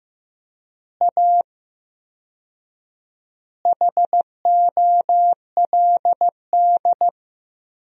Morse code, heard as a keyed tone, about 15 wpm, 700 Hz. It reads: A HOLD
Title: 4x Word-Spacing - Sentences from Top 400 Words - 15wpm